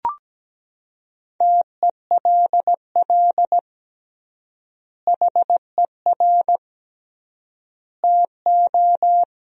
Morse code, heard as a keyed tone, about 17 wpm, 700 Hz. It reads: TELL HER TO
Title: 3x Word-Spacing - Sentences from Top 300 Words - 17wpm